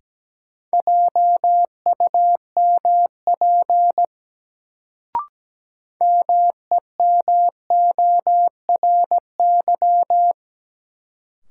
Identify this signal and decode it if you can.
Morse code, heard as a keyed tone, about 17 wpm, 700 Hz. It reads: JUMP MEMORY
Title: Sentences from Top 1300 Words - New Word Drill - 17wpm